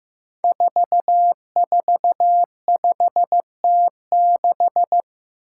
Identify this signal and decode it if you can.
Morse code, heard as a keyed tone, about 15 wpm, 700 Hz. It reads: 445T6
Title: Five Challenging Letter-Number Combinations 15wpm